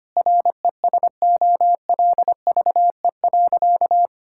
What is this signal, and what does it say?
Morse code, heard as a keyed tone, about 25 wpm, 700 Hz. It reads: RESOLVE.
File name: practice / 25wpm / Difficult Sentences with Hints - 25wpm